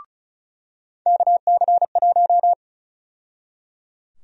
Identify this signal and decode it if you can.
Morse code, heard as a keyed tone, about 34 wpm, 700 Hz. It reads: KC1
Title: USA Call Sign Prefixes - 35wpm